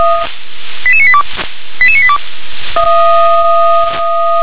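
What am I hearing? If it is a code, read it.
Five-tone selective call (ZVEI), heard as quick runs of five tones, tones of about 70 ms. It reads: 88982 88982
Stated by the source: ZVEI five-tone selective call received on an SDR.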